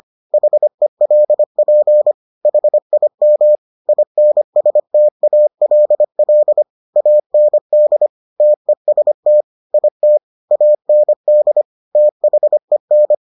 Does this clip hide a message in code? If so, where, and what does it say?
Yes: Morse code from 0.3 to 13.1 s, heard as a keyed tone, about 25 wpm, 600 Hz. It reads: HELP HIM INSTALL AND TEST IT AND THEN